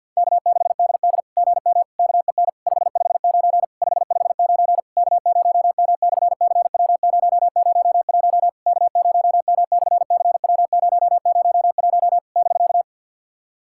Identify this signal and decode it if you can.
Morse code, heard as a keyed tone, about 51 wpm, 690 Hz. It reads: K6DD GM KEN 559 559 K0M/CW001 K0M/CW001 <BK>
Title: SOTA QSO Practice 50wpm